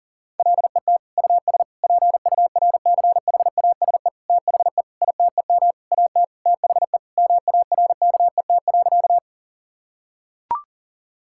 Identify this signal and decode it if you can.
Morse code, heard as a keyed tone, about 40 wpm, 700 Hz. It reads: LET US PURCHASE THE ITEM AT THE MARKET.
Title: Sentences from Top 1000 Words - Review Part 6 - 40wpm